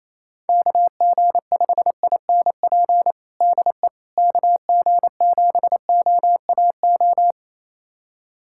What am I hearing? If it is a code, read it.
Morse code, heard as a keyed tone, about 27 wpm, 700 Hz. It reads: KG5INP DE KG7OAO
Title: USA Callsign DE Callsign - 6 Characters - 28wpm